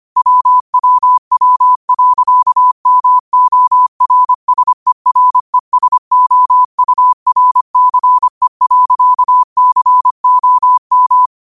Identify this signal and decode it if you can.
Morse code, heard as a keyed tone, about 25 wpm, 1000 Hz. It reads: WWW.MORSERESOURCE.COM
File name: qotdfiles / 25wpm